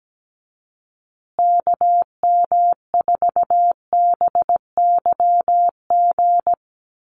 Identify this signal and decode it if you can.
Morse code, heard as a keyed tone, about 17 wpm, 700 Hz. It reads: KM4BYG
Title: USA Call Signs Repeated 3x 17wpm